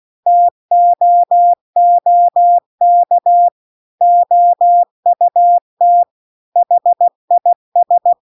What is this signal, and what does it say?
Morse code, heard as a keyed tone, about 16 wpm, 700 Hz. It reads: TOOK OUT HIS